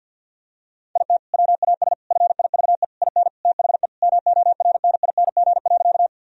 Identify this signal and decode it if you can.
Morse code, heard as a keyed tone, about 47 wpm, 700 Hz. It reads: IT WAS FIVE IN THE MORNING.
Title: Sentences from Top 1000 Words - Review Part 2 - 50wpm